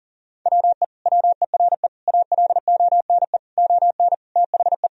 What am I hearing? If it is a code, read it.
Morse code, heard as a keyed tone, about 37 wpm, 700 Hz. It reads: WE WERE ALONE ON THE
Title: Sentences from Top 1000 Words - Review Part 3 - 40wpm